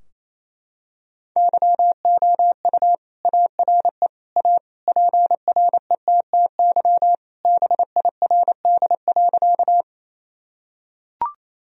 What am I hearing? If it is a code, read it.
Morse code, heard as a keyed tone, about 28 wpm, 700 Hz. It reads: YOU ARE A PRETTY BIRD.
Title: Sentences from Top 1000 Words - Review Part 2 - 28wpm